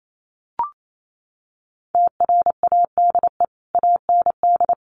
Morse code, heard as a keyed tone, about 28 wpm, 700 Hz. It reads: TRADE AND
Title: Sentences from Top 1500 Words 28wpm